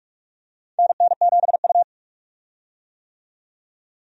Morse code, heard as a keyed tone, about 41 wpm, 700 Hz. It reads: NN7U
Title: USA Call Signs - 4 Characters - 45wpm